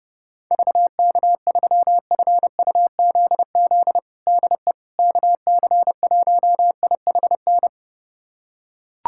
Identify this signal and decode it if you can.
Morse code, heard as a keyed tone, about 32 wpm, 700 Hz. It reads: VK3FUZZ DE KC1IHN
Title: USA Callsign DE Callsign 30wpm